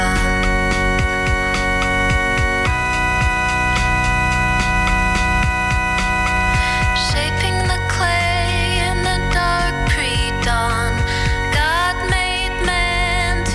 Dial-up modem handshake.